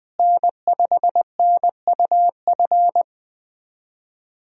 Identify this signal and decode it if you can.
Morse code, heard as a keyed tone, about 20 wpm, 700 Hz. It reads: N5NUF
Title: USA Call Signs - 5 Characters - Repeated 3x - 20wpm